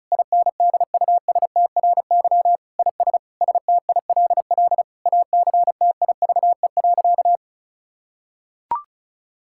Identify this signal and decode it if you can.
Morse code, heard as a keyed tone, about 36 wpm, 700 Hz. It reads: INDUSTRY IS STILL ACTIVE.
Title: Sentences from Top 1000 Words - Review Part 6 - 35wpm